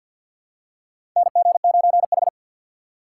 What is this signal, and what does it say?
Morse code, heard as a keyed tone, about 53 wpm, 700 Hz. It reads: NG9H
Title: USA Call Signs - 4 Characters - 2x Character Spacing - 50wpm